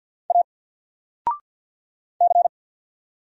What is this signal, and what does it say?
Morse code, heard as a keyed tone, about 50 wpm, 700 Hz. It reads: A C
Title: Single Letter-Number Alt 50wpm